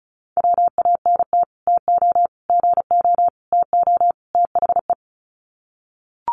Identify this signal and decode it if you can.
Morse code, heard as a keyed tone, about 34 wpm, 700 Hz. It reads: WANT TO GO TO THE
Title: Sets of 5 Words Encore2 35wpm